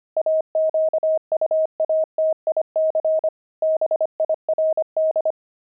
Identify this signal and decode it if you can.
Morse code, heard as a keyed tone, about 25 wpm, 630 Hz. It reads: AQUATIC BIRD
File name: pangram-fs-25-cs-25-journey